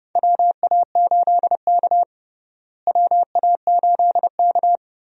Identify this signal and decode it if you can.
Morse code, heard as a keyed tone, about 30 wpm, 700 Hz. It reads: WA8K WA8K
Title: USA Call Signs - 4 Characters - Repeated 2x - 30wpm